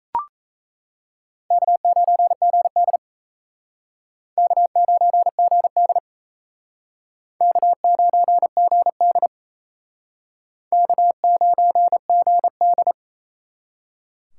Morse code, heard as a keyed tone, about 36 wpm, 700 Hz. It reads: K9GD K9GD K9GD K9GD